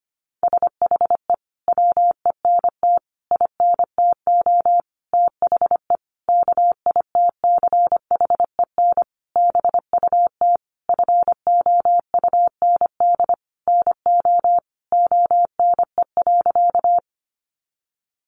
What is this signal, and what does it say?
Morse code, heard as a keyed tone, about 25 wpm, 700 Hz. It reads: SHE WENT INTO THE KITCHEN BUT FOUND NO ONE.
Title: Sentences from Top 1400 Words 25wpm